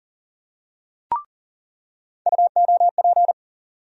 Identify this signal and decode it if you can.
Morse code, heard as a keyed tone, about 40 wpm, 700 Hz. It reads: UOP
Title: USA Call Sign Postfixes - 3 Letters 40wpm